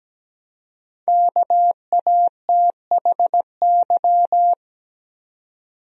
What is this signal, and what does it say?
Morse code, heard as a keyed tone, about 17 wpm, 700 Hz. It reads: KATHY
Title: USA Common Names - Max 5 Letters - 17wpm